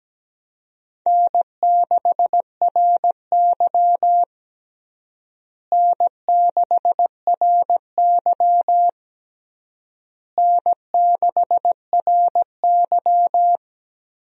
Morse code, heard as a keyed tone, about 17 wpm, 700 Hz. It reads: N6RY N6RY N6RY
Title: USA Call Signs - 4 Characters - Repeated 3x - 17wpm